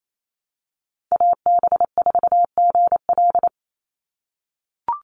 Morse code, heard as a keyed tone, about 28 wpm, 700 Hz. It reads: AB4GL
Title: USA Call Signs 28wpm